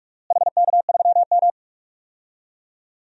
Morse code, heard as a keyed tone, about 46 wpm, 700 Hz. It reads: HK3M
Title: DXCC-100 Call Signs - 4 Characters - Repeated 2x - 45wpm